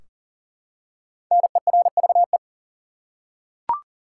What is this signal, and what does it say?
Morse code, heard as a keyed tone, about 40 wpm, 700 Hz. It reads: NERVE
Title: Five Letter Words - 40wpm